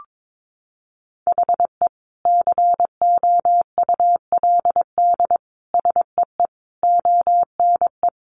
Morse code, heard as a keyed tone, about 22 wpm, 700 Hz. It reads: HE COULD SEE ONE
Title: Sentences from Top 1100 Words 22wpm